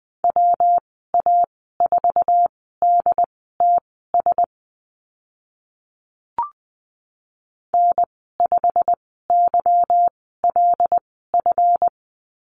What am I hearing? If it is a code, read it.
Morse code, heard as a keyed tone, about 20 wpm, spaced as for 15 wpm, 700 Hz. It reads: WA4DTS N5YLF
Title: 147 Lesson 24g - Letter y - Call Signs 20wpm